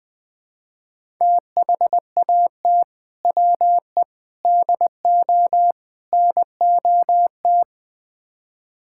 Morse code, heard as a keyed tone, about 20 wpm, 700 Hz. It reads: THAT WE DO NOT
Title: Build A Phrase - 4-Word Phrases - 20wpm